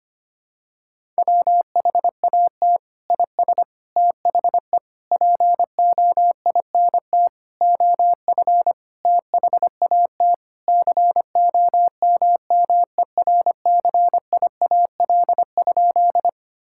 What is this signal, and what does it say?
Morse code, heard as a keyed tone, about 25 wpm, 700 Hz. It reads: WHAT IS THE POINT OF THAT COMMERCIAL?
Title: Sentences from Top 1000 Words - Review Part 5 - 25wpm